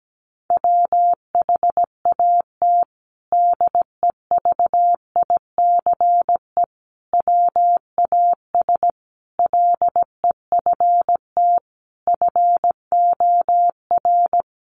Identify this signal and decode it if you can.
Morse code, heard as a keyed tone, about 17 wpm, 700 Hz. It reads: WHAT DEVICE WAS LEFT FOR
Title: Sentences from Top 1500 Words 17wpm